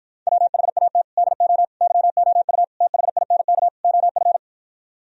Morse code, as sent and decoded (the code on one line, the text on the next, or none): .-- .... .- - / -.. --- / -.-- --- ..- / - .... .. -. -.- / --- ..-.
WHAT DO YOU THINK OF